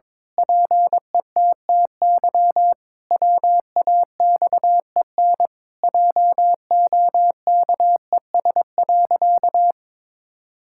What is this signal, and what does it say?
Morse code, heard as a keyed tone, about 22 wpm, 700 Hz. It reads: PETTY WAXEN JOKES.
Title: Pangram Challenge v2 - 22wpm